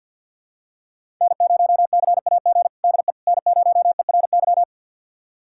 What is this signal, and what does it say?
Morse code, heard as a keyed tone, about 50 wpm, 700 Hz. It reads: N0CAG DE N0ERY
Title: USA Callsign DE Callsign - 5 Characters - 50wpm